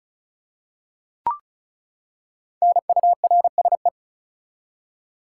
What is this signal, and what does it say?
Morse code, heard as a keyed tone, about 36 wpm, 700 Hz. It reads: NURSE
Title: Five Letter Words - 35wpm